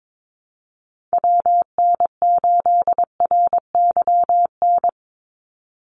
Morse code, heard as a keyed tone, about 22 wpm, 700 Hz. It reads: WN8RYN
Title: USA Call Signs 22wpm